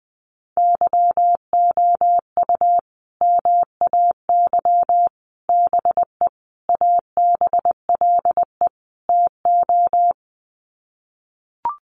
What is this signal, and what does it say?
Morse code, heard as a keyed tone, about 20 wpm, 700 Hz. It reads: YOU MAY BE ABLE TO
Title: Sets of 5 Words Encore 20wpm